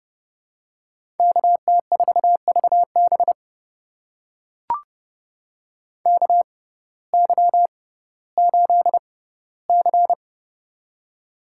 Morse code, heard as a keyed tone, about 30 wpm, spaced as for 17 wpm, 700 Hz. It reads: KT4VB KY8C
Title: USA Call Signs - 6x Character Spacing - 30wpm